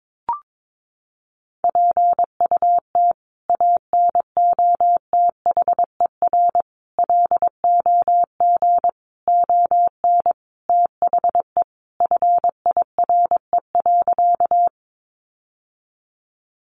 Morse code, heard as a keyed tone, about 22 wpm, 700 Hz. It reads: PUT ANOTHER LOG ON THE FIRE.
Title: Sentences from Top 1400 Words 22wpm